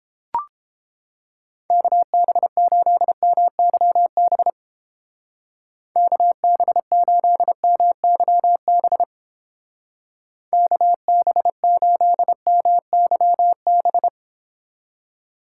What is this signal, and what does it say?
Morse code, heard as a keyed tone, about 32 wpm, 700 Hz. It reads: KB8MYB KB8MYB KB8MYB